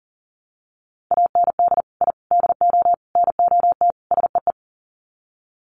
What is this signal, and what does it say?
Morse code, heard as a keyed tone, about 40 wpm, 700 Hz. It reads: AND I DO NOT SEE
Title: Sets of 5 Words Encore2 40wpm